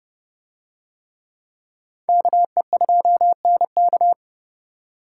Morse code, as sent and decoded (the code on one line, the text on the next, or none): -.- . ..--- -. -.-
KE2NK